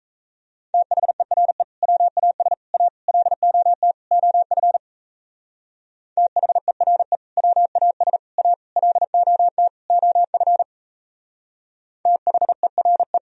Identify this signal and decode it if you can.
Morse code, heard as a keyed tone, about 34 wpm, 700 Hz. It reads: THERE WAS A LOT OF THERE WAS A LOT OF THERE
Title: Speed-Racing - Sets of 5 Words 28wpm